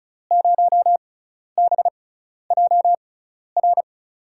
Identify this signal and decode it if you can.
Morse code, heard as a keyed tone, about 35 wpm, spaced as for 12 wpm, 700 Hz. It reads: 0BJR
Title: USA Call Signs - 5 Characters - 6x Character Spacing - 35wpm